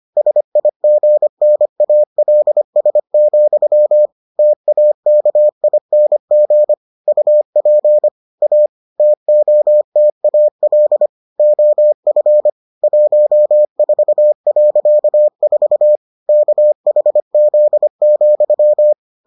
Morse code, heard as a keyed tone, about 25 wpm, 600 Hz. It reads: SIGNALS, TAKING UP A TOTAL OF 14.4 KHZ,